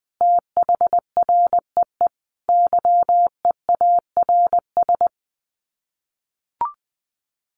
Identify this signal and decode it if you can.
Morse code, heard as a keyed tone, about 19 wpm, 700 Hz. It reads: THREE YEARS
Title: Sets of 5 Words Encore2 20wpm